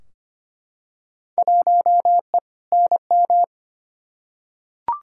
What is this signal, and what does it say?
Morse code, heard as a keyed tone, about 25 wpm, 700 Hz. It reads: 1E NM
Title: Field Day Exchanges - 25wpm